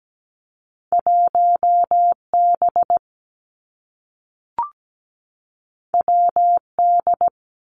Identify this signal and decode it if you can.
Morse code, heard as a keyed tone, about 17 wpm, 700 Hz. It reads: 1 B W D
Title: Two Challenging Letter-Number Combinations 17wpm